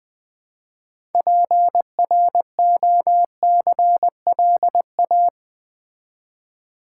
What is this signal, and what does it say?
Morse code, heard as a keyed tone, about 20 wpm, 700 Hz. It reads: PROCLA
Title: Build A Word - 2 Letters At A Time - 8-Letter Words - 20wpm